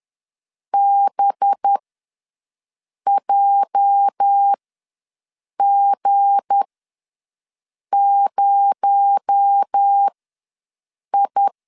Morse code, heard as a keyed tone, about 10 wpm, spaced as for 5 wpm, 800 Hz. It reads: BJG0I
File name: LEZIONI / 8 YJ45 / 8 YJ ws 10wpm 4043